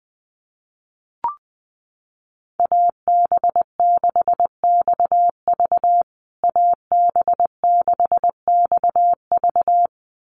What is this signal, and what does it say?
Morse code, heard as a keyed tone, about 20 wpm, 700 Hz. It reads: AB6XV AB6XV
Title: USA Call Signs Repeated 2x 20wpm